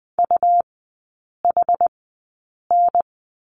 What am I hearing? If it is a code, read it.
Morse code, heard as a keyed tone, about 20 wpm, spaced as for 8 wpm, 700 Hz. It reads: UHN
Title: 100 Lesson 17e - Letter u - Three Characters 20wpm